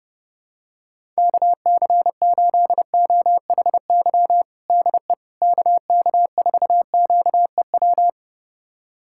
Morse code, heard as a keyed tone, about 32 wpm, 700 Hz. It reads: KC8OHY DE KK4QEW
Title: USA Callsign DE Callsign 30wpm